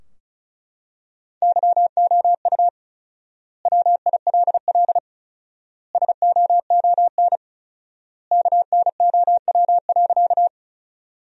Morse code, heard as a keyed tone, about 36 wpm, 700 Hz. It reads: YOU WILL SOON KNOW.
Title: 4x Word-Spacing - Sentences from Top 400 Words - 35wpm